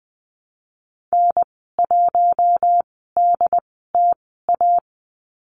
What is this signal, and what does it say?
Morse code, heard as a keyed tone, about 20 wpm, spaced as for 15 wpm, 700 Hz. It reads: N1DTA
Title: 088 Lesson 15g - Number 5 - Call Signs 20wpm